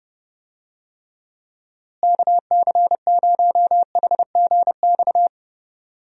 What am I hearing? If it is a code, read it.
Morse code, heard as a keyed tone, about 30 wpm, 700 Hz. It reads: KC0HGX
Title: USA Call Signs - 6 Characters - Repeated 3x - 30wpm